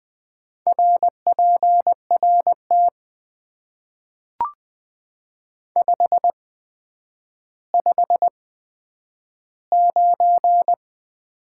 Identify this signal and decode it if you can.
Morse code, heard as a keyed tone, about 20 wpm, spaced as for 10 wpm, 700 Hz. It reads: RPRT 559
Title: Top 90 QSO Elements - 8x Character Spacing - 20wpm